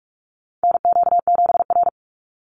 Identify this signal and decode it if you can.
Morse code, heard as a keyed tone, about 44 wpm, 700 Hz. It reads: NQ7R
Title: NA SOTA Chaser Callsigns 45wpm